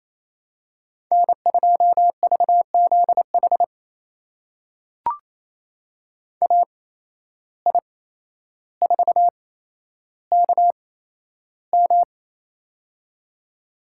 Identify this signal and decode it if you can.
Morse code, heard as a keyed tone, about 28 wpm, spaced as for 12 wpm, 700 Hz. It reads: N2VZH AI4KM
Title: USA Call Signs - 5 Characters - 8x Character Spacing - 28wpm